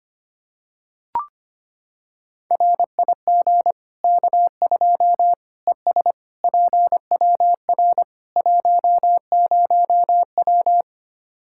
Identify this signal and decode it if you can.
Morse code, heard as a keyed tone, about 25 wpm, 700 Hz. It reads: RIG K2 ES PWR 10W
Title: Practice QSO - Exchange 2 - Info - 25wpm